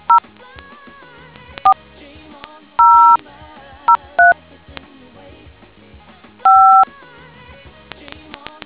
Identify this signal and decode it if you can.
DTMF, heard as telephone touch tones, tones of about 140 ms. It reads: 0 4 003 5